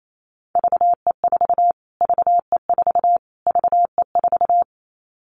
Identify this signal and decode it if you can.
Morse code, heard as a keyed tone, about 28 wpm, 700 Hz. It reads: VE4 VE4 VE4
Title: Callsign Prefixes for Top 25 DXCC - Repeated 3x - 28wpm